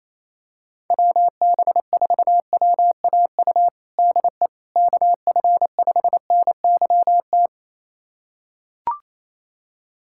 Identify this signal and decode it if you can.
Morse code, heard as a keyed tone, about 27 wpm, 700 Hz. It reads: WB4WAU DE KF5NYT
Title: USA Callsign DE Callsign - 6 Characters - 28wpm